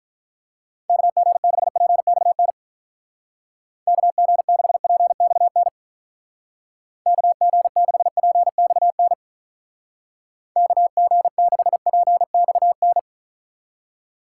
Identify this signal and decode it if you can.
Morse code, heard as a keyed tone, about 44 wpm, 700 Hz. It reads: KG6PXN KG6PXN KG6PXN KG6PXN
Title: Speed Racing - USA 2x3 Call Signs 35wpm